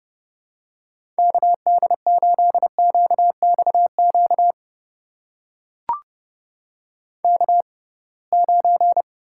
Morse code, heard as a keyed tone, about 32 wpm, spaced as for 23 wpm, 700 Hz. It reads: KD8QXQ K9
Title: USA Call Signs - 6x Character Spacing - 30wpm